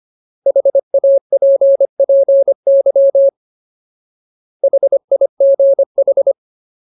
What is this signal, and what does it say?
Morse code, heard as a keyed tone, about 25 wpm, 550 Hz. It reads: HAPPY HIGH
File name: Ultimate Word List / 25WPM Ultimate